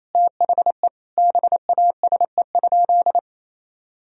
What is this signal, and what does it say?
Morse code, heard as a keyed tone, about 28 wpm, 700 Hz. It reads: THE BASE?
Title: Sentences from Top 1000 Words - Review Part 1 - 28wpm